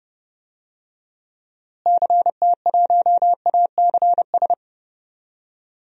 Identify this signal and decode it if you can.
Morse code, heard as a keyed tone, about 30 wpm, 700 Hz. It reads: CT1ACS
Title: DXCC-100 Call Signs - 6 Characters - Repeated 2x - 30wpm